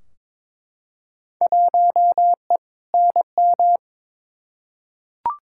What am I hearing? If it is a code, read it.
Morse code, heard as a keyed tone, about 23 wpm, 700 Hz. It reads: 1E NM
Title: Field Day Exchanges - 22wpm